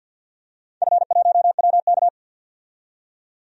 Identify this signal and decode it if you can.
Morse code, heard as a keyed tone, about 51 wpm, 700 Hz. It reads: F1WK
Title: DXCC-100 Call Signs - 4 Characters - 50wpm